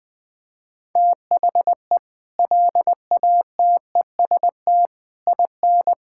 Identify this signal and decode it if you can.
Morse code, heard as a keyed tone, about 20 wpm, 700 Hz. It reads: THE LATEST IN
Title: Sets of 6 Words - 20wpm